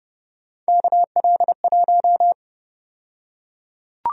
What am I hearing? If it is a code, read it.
Morse code, heard as a keyed tone, about 32 wpm, 700 Hz. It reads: KL1
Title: USA Call Sign Prefixes - 30wpm